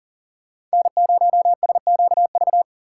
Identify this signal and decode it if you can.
Morse code, heard as a keyed tone, about 40 wpm, 700 Hz. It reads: N0SQV
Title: USA Call Signs - 5 Characters - 2x Character Spacing - 40wpm